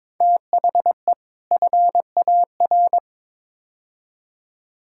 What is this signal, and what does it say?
Morse code, heard as a keyed tone, about 22 wpm, 700 Hz. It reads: THE FAR
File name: practice / 22wpm / Build A Phrase - 5-Word Phrases - 22wpm